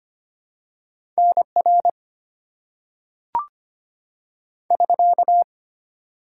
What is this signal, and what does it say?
Morse code, heard as a keyed tone, about 25 wpm, 700 Hz. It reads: NR <SK>
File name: Top 90 QSO Elements - 4x Character Spacing - 25wpm